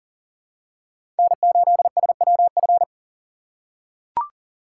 Morse code, heard as a keyed tone, about 40 wpm, 700 Hz. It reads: N8SWF
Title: USA Call Signs - 4x Character Spacing - 40wpm